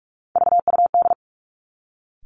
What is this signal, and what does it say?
Morse code, heard as a keyed tone, about 46 wpm, 700 Hz. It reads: VUD
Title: Three Challenging Letter-Number Combinations 45wpm